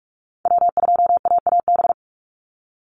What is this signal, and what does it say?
Morse code, heard as a keyed tone, about 44 wpm, 700 Hz. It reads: W2AAB
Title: Call Signs Repeated 2x - 45wpm